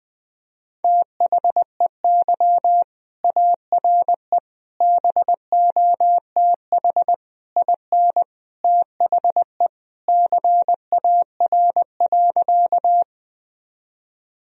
Morse code, heard as a keyed tone, about 20 wpm, 700 Hz. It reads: THEY ARE BOTH IN THE CAR.
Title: Sentences from Top 1000 Words - Review Part 1 - 20wpm